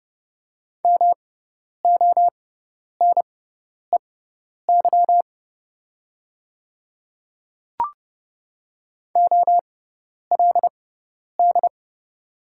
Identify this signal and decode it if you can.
Morse code, heard as a keyed tone, about 30 wpm, spaced as for 10 wpm, 700 Hz. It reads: MONEY OLD
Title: 6x Character Spacing - Top 500 Words - 30wpm